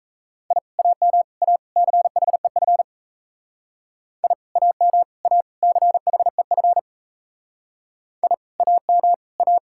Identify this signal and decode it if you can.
Morse code, heard as a keyed tone, about 35 wpm, 700 Hz. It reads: I AM A CHEF I AM A CHEF I AM A